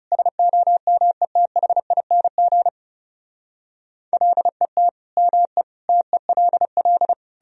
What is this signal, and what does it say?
Morse code, heard as a keyed tone, about 40 wpm, 700 Hz. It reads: SOMETHING LET ME TELL